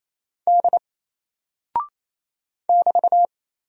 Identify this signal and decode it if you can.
Morse code, heard as a keyed tone, about 28 wpm, 700 Hz. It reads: D <BT>
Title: Single Character (Letter:Number:Punctuation:Prosign) Alt 28wpm